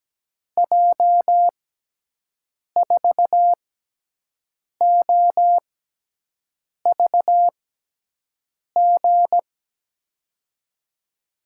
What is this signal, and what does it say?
Morse code, heard as a keyed tone, about 17 wpm, spaced as for 6 wpm, 700 Hz. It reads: J4OVG
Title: USA Call Signs - 6 Characters - 6x Character Spacing - 17wpm